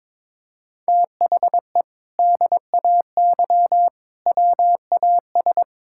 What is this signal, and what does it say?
Morse code, heard as a keyed tone, about 22 wpm, 700 Hz. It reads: THE DAY WAS
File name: practice / 22wpm / Sentences with 2-Word Phrases - 22wpm